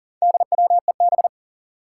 Morse code, heard as a keyed tone, about 43 wpm, 700 Hz. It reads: DWEB
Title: Four Challenging Letter-Number Combinations 40wpm